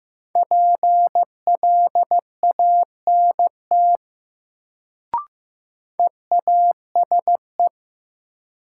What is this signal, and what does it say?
Morse code, heard as a keyed tone, about 15 wpm, 700 Hz. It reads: PLANT EASE
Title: Sentences from Top 1000 Words - Review Part 1 - New Word Drill - 15wpm